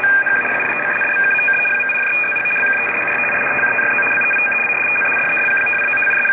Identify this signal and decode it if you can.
Two-tone teleprinter signal, about 850 Hz shift.